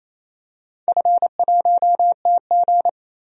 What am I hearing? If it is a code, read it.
Morse code, heard as a keyed tone, about 28 wpm, 700 Hz. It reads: F1TG
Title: DXCC-100 Call Signs - 4 Characters - 28wpm